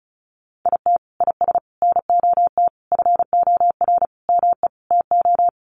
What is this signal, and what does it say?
Morse code, heard as a keyed tone, about 34 wpm, 700 Hz. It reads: IT IS NOT FOR ME TO
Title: Sentences from Top 600 Words 35wpm